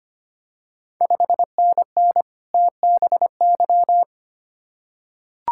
Morse code, heard as a keyed tone, about 25 wpm, 700 Hz. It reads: 5NN TBY
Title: Ontario QSO Party Exchange 25wpm